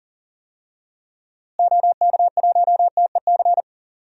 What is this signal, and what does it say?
Morse code, heard as a keyed tone, about 40 wpm, 700 Hz. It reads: OK1TEC
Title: DXCC-100 Call Signs - 6 Characters - Repeated 3x - 40wpm